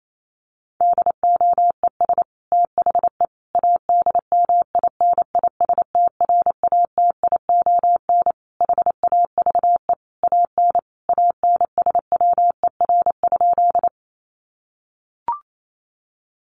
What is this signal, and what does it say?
Morse code, heard as a keyed tone, about 28 wpm, 700 Hz. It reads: DOES THE ADMINISTRATION HAVE AN ANSWER?
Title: Sentences from Top 1300 Words 28wpm